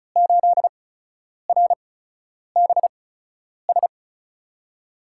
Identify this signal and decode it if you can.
Morse code, heard as a keyed tone, about 34 wpm, spaced as for 10 wpm, 700 Hz. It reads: 8RBS
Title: USA Call Signs - 5 Characters - 8x Character Spacing - 35wpm